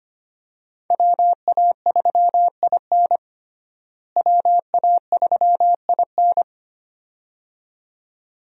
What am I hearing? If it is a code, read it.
Morse code, heard as a keyed tone, about 25 wpm, 700 Hz. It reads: WA3IN WA3IN
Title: USA Call Signs - 5 Characters - Repeated 2x - 25wpm